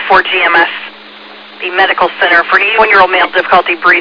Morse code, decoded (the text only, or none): none